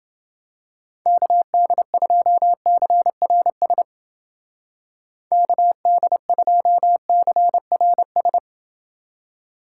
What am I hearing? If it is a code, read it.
Morse code, heard as a keyed tone, about 29 wpm, 700 Hz. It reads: KD2CRS KD2CRS